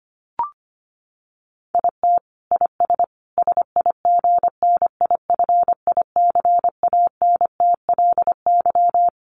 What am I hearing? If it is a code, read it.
Morse code, heard as a keyed tone, about 25 wpm, 700 Hz. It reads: IT IS SIGNIFICANTLY
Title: Sentences from Top 1500 Words 25wpm